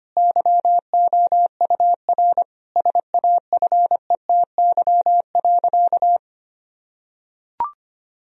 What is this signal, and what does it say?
Morse code, heard as a keyed tone, about 25 wpm, 700 Hz. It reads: YOUR SAFETY.
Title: Sentences from Top 1000 Words - Review Part 4 - 25wpm